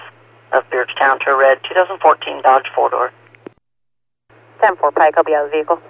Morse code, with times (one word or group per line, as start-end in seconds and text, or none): none